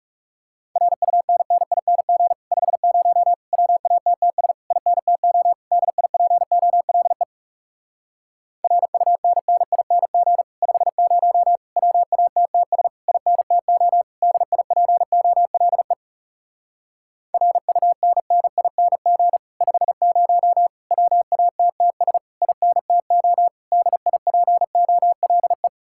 Morse code, heard as a keyed tone, about 40 wpm, 700 Hz. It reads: RUNNING 50 WATTS INTO DIPOLE RUNNING 50 WATTS INTO DIPOLE RUNNING 50 WATTS INTO DIPOLE